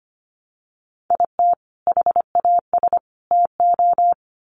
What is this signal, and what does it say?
Morse code, heard as a keyed tone, about 25 wpm, 700 Hz. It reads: IT HAS TO
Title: Sets of 5 Words 25wpm